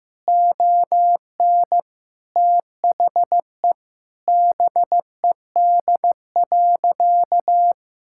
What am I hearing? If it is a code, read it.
Morse code, heard as a keyed tone, about 15 wpm, 700 Hz. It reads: ON THE BED.
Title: Sentences from Top 1600 Words 15wpm